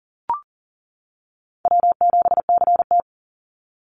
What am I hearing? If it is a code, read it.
Morse code, heard as a keyed tone, about 40 wpm, 700 Hz. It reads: W7CT
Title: Easy Call Signs - 40wpm